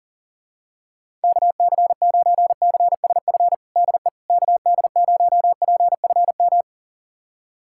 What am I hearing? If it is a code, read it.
Morse code, heard as a keyed tone, about 41 wpm, 690 Hz. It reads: KC9CSF DE KD0PFM
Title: USA Callsign DE Callsign - 6 Characters - 40wpm